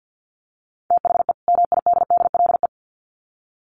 Morse code, heard as a keyed tone, about 50 wpm, 700 Hz. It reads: THE MIDDLE
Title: Sets of 2 Words 50wpm